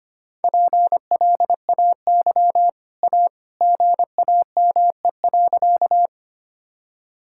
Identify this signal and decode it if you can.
Morse code, heard as a keyed tone, about 25 wpm, 700 Hz. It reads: PLAY A GAME.
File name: Easy Sentences with Hints - 25wpm